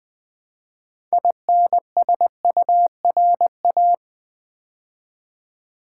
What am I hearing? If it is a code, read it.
Morse code, heard as a keyed tone, about 20 wpm, 700 Hz. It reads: INSURA
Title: Build A Word - 3 Letters At A Time - 9-Letter Words - 20wpm